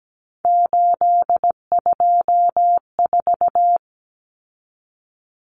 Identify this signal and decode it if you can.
Morse code, heard as a keyed tone, about 17 wpm, 700 Hz. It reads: 824
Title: Common Cut Numbers - 3-Digit 17wpm